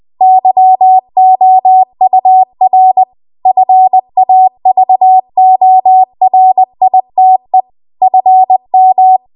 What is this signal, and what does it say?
Morse code, heard as a keyed tone, about 20 wpm, 750 Hz. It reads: YOUR FAVORITE FM
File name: Archive / 20 WPM / 20WPM